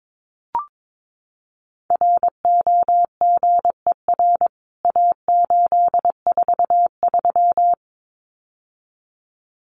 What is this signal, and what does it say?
Morse code, heard as a keyed tone, about 22 wpm, 700 Hz. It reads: ROGER A843
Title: CWT Exchange 22wpm